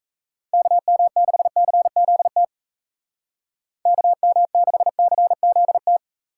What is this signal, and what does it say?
Morse code, heard as a keyed tone, about 36 wpm, 700 Hz. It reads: KM6CZT KM6CZT